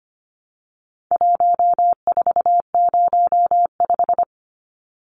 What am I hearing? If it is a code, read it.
Morse code, heard as a keyed tone, about 25 wpm, 700 Hz. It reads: 1405
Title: Numbers - 4-Digit 25wpm